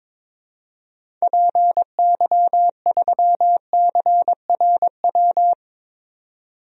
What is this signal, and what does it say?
Morse code, heard as a keyed tone, about 22 wpm, 700 Hz. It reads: PY3CRW
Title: DXCC-100 Call Signs - 6 Characters - 22wpm